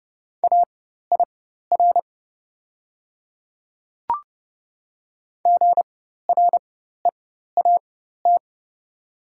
Morse code, heard as a keyed tone, about 31 wpm, spaced as for 14 wpm, 700 Hz. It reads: AIR GREAT
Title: 4x Character Spacing - Top 300 Words - 30wpm